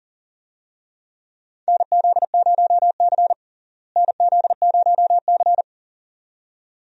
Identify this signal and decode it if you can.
Morse code, heard as a keyed tone, about 40 wpm, 700 Hz. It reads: NZ0C NZ0C
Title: USA Call Signs - 4 Characters - Repeated 2x - 40wpm